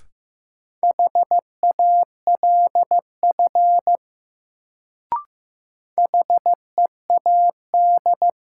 Morse code, heard as a keyed tone, about 15 wpm, 700 Hz. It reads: HALF HEAD
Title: Sets of 3 Words - New Word Drill - 15wpm